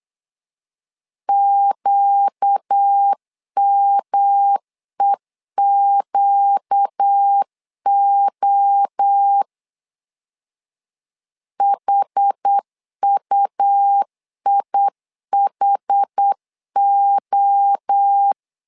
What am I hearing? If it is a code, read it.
Morse code, heard as a keyed tone, about 8 wpm, 800 Hz. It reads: QMEQO HUIHO
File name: LEZIONI / 4 Q 8wpm 2732